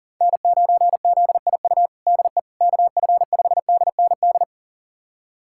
Morse code, heard as a keyed tone, about 40 wpm, 700 Hz. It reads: N9ZIU DE KF5DND
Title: USA Callsign DE Callsign 40wpm